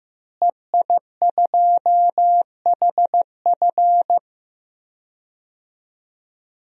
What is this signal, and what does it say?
Morse code, heard as a keyed tone, about 15 wpm, 700 Hz. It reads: EI2HF
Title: DXCC-100 Call Signs - 5 Characters - Repeated 2x - 15wpm